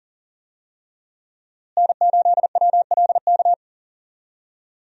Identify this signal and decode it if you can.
Morse code, heard as a keyed tone, about 40 wpm, 700 Hz. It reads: N8WLK
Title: USA Call Signs - 5 Characters - Repeated 3x - 40wpm